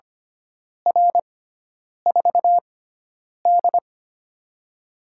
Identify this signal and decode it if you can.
Morse code, heard as a keyed tone, about 25 wpm, spaced as for 9 wpm, 700 Hz. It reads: R4D
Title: USA Call Signs - 4 Characters - 6x Character Spacing - 25wpm